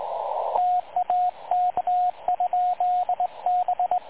Morse code, heard as a keyed tone, about 29 wpm, spaced as for 24 wpm, 720 Hz, with radio noise. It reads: TAK ?B
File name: dlazaawansowanych / nagraniazpasma / websdr recording 3565.0kHz